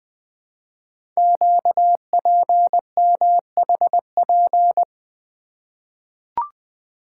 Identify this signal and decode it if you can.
Morse code, heard as a keyed tone, about 20 wpm, 700 Hz. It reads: QPMHP
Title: Code Groups - 5 Letters - 20wpm